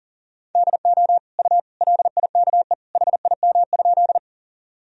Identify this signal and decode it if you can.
Morse code, heard as a keyed tone, about 40 wpm, 700 Hz. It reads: DO U LIKE HIM?
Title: Sentences from Top 100 Words and Abbreviations 40wpm